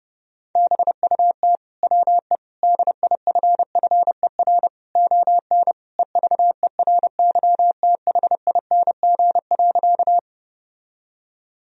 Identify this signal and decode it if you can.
Morse code, heard as a keyed tone, about 30 wpm, 700 Hz. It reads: BUT WE DIFFER ON EVERYTHING.
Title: Sentences from Top 1000 Words - Review Part 1 - 30wpm